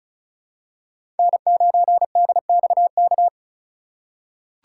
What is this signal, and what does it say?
Morse code, heard as a keyed tone, about 36 wpm, 700 Hz. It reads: N9DXK
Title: USA Call Signs - 5 Characters - 2x Character Spacing - 35wpm